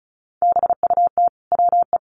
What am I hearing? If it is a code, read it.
Morse code, heard as a keyed tone, about 34 wpm, 700 Hz. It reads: BUT WE
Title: Sentences from Top 1500 Words 35wpm